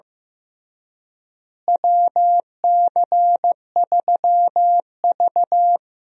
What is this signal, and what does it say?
Morse code, heard as a keyed tone, about 15 wpm, 700 Hz. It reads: WC3V
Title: USA Call Signs - 4 Characters - Repeated 3x - 15wpm